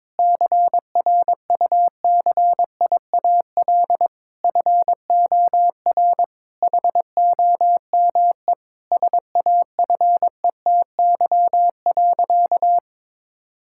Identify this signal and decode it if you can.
Morse code, heard as a keyed tone, about 22 wpm, 700 Hz. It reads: CRUCIAL FOR HOME SAFETY.
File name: Difficult Sentences with Hints - 22wpm